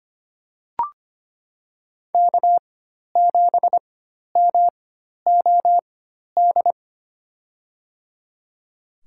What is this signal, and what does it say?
Morse code, heard as a keyed tone, about 25 wpm, spaced as for 12 wpm, 690 Hz. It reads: K7MOD
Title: USA Call Signs - 4x Character Spacing - 25wpm